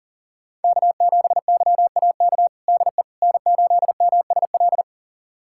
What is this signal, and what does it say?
Morse code, heard as a keyed tone, about 40 wpm, 700 Hz. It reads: K7YAK DE N8MSL
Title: USA Callsign DE Callsign - 5 Characters - 40wpm